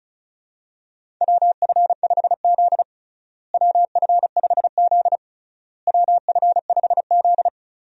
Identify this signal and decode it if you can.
Morse code, heard as a keyed tone, about 34 wpm, 700 Hz. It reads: WF5Z WF5Z WF5Z
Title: USA Call Signs - 4 Characters - Repeated 3x - 35wpm